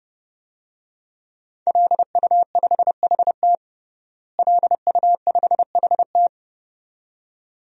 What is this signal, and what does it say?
Morse code, heard as a keyed tone, about 30 wpm, 700 Hz. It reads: LU5HT LU5HT
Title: DXCC-100 Call Signs - 5 Characters - Repeated 2x - 30wpm